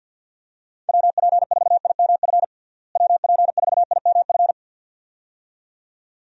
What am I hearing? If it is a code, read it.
Morse code, heard as a keyed tone, about 49 wpm, 700 Hz. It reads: WP4IMF WP4IMF
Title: USA Call Signs - 6 Characters - Repeated 2x - 50wpm